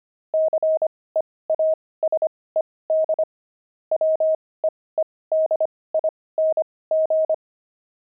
Morse code, heard as a keyed tone, about 25 wpm, spaced as for 18 wpm, 630 Hz. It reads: CEASED WEEDING
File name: pangram-fs-18-cs-25-beehive